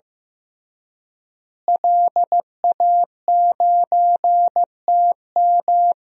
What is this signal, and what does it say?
Morse code, heard as a keyed tone, about 15 wpm, 700 Hz. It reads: LA9TM
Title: DXCC-100 Call Signs - 5 Characters - Repeated 2x - 15wpm